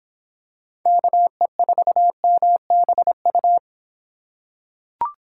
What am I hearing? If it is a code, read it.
Morse code, heard as a keyed tone, about 26 wpm, 700 Hz. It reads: KE4MBU